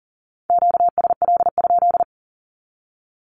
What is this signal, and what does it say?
Morse code, heard as a keyed tone, about 38 wpm, 700 Hz. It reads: QSL?
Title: Q-Codes with Spoken Meaning - 40wpm